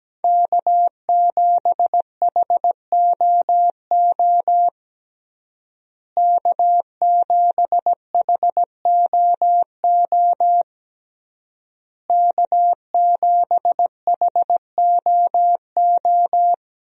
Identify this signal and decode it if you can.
Morse code, heard as a keyed tone, about 17 wpm, 700 Hz. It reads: K7HOO K7HOO K7HOO
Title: USA Call Signs - 5 Characters - Repeated 3x - 17wpm